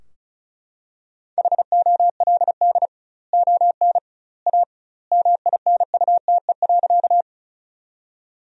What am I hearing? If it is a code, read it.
Morse code, heard as a keyed tone, about 34 wpm, 700 Hz. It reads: HOLD ON A MINUTE.
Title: 2x Word-Spacing - Sentences from Top 500 Words - 35wpm